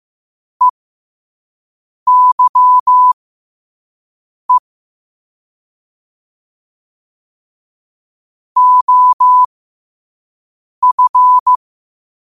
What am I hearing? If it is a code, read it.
Morse code, heard as a keyed tone, about 15 wpm, spaced as for 5 wpm, 1000 Hz. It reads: EYE OF
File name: qotdfiles / 5wpm